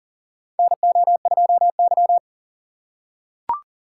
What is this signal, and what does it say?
Morse code, heard as a keyed tone, about 38 wpm, 700 Hz. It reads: NO2Y
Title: USA Call Signs - 4x Character Spacing - 40wpm